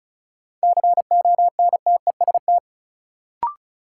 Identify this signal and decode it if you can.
Morse code, heard as a keyed tone, about 36 wpm, 700 Hz. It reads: CONTEST
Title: Sentences from Top 1600 Words - New Word Drill - 35wpm